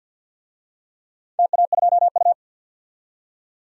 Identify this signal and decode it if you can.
Morse code, heard as a keyed tone, about 51 wpm, 700 Hz. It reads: TA2U
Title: DXCC-100 Call Signs - 4 Characters - 50wpm